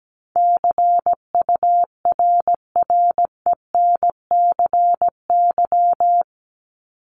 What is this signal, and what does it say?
Morse code, heard as a keyed tone, about 17 wpm, 700 Hz. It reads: CURRENCY
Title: Top 1251-1500 Words 17wpm